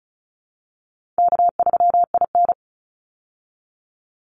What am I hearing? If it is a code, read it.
Morse code, heard as a keyed tone, about 34 wpm, 700 Hz. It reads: K3IN
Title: Easy Call Signs Repeated 2x - 35wpm